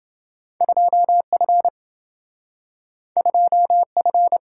Morse code, heard as a keyed tone, about 27 wpm, 700 Hz. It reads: 2F 2F